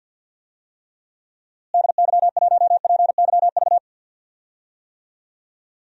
Morse code, heard as a keyed tone, about 52 wpm, 700 Hz. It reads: DY1PYV
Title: DXCC-100 Call Signs - 6 Characters - Repeated 3x - 50wpm